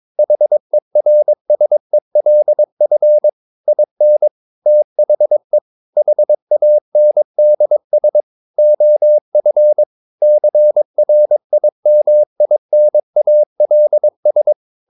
Morse code, heard as a keyed tone, about 22 wpm, 600 Hz. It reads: HERSELF IN THE HANDS OF CRIMINALS
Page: CW med Gnister